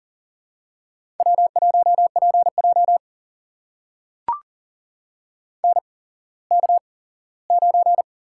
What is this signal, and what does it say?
Morse code, heard as a keyed tone, about 38 wpm, spaced as for 19 wpm, 700 Hz. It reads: W1PJ NK9
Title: USA Call Signs - 4 Characters - 8x Character Spacing - 40wpm